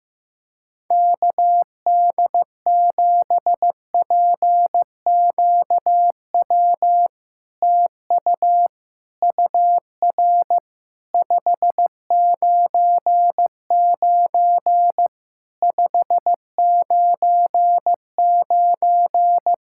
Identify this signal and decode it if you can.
Morse code, heard as a keyed tone, about 15 wpm, 690 Hz. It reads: KD7PQW TU UR 599 599
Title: SOTA QSO Practice 15wpm